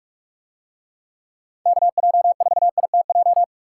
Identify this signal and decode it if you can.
Morse code, heard as a keyed tone, about 42 wpm, 700 Hz. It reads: KJ4ITJ